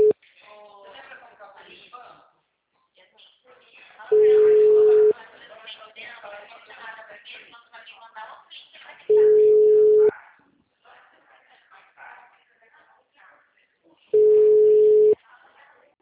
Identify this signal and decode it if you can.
Ringback tone (Europe, 425 Hz), with speech.